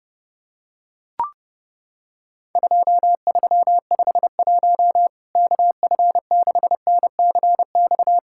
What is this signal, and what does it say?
Morse code, heard as a keyed tone, about 32 wpm, 700 Hz. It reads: 2351 KF6NCX
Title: Worked All Europe DX Contest - QTC Exchange - 30wpm